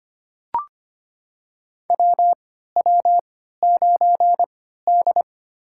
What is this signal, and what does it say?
Morse code, heard as a keyed tone, about 25 wpm, spaced as for 14 wpm, 700 Hz. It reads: WW9D
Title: USA Call Signs - 3x Character Spacing - 25wpm